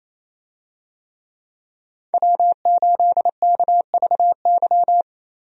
Morse code, heard as a keyed tone, about 28 wpm, 700 Hz. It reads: W8KVY
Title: USA Call Signs - 5 Characters - Repeated 2x - 28wpm